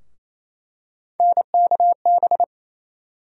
Morse code, heard as a keyed tone, about 28 wpm, 700 Hz. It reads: NKB
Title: USA Call Sign Postfixes - 3 Letters 28wpm